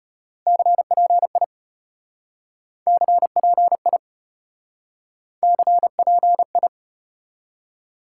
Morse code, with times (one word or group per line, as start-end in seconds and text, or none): 0.5-1.4 CPI
2.9-4.0 CPI
5.4-6.7 CPI